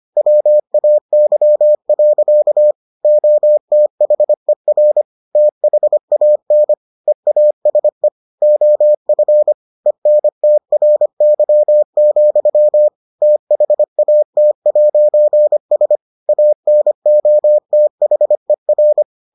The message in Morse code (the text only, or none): WAY. OTHER THAN EASE OF ENTRY, THAT'S ANOTHER